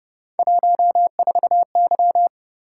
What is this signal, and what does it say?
Morse code, heard as a keyed tone, about 29 wpm, 700 Hz. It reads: 14Y
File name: Speed-Racing - Three Letter-Number Combinations - 25wpm